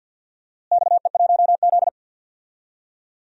Morse code, heard as a keyed tone, about 52 wpm, 700 Hz. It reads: XE1Z
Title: DXCC-100 Call Signs - 4 Characters - Repeated 2x - 50wpm